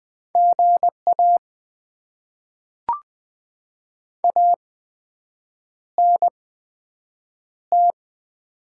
Morse code, heard as a keyed tone, about 20 wpm, spaced as for 7 wpm, 700 Hz. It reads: GA ANT
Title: Top 20 QSO Elements - 8x Character Spacing - 20wpm